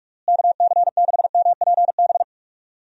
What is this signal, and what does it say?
Morse code, heard as a keyed tone, about 42 wpm, 700 Hz. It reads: KC6MPB
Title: USA Call Signs - 6 Characters - Repeated 3x - 45wpm